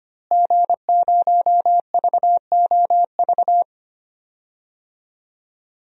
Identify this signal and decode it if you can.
Morse code, heard as a keyed tone, about 25 wpm, 700 Hz. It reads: G0VOV
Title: DXCC-100 Call Signs - 5 Characters - Repeated 2x - 25wpm